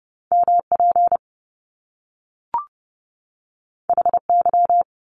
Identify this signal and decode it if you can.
Morse code, heard as a keyed tone, about 30 wpm, 700 Hz. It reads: MP HY
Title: Top Two Letter Combinations in Words 30wpm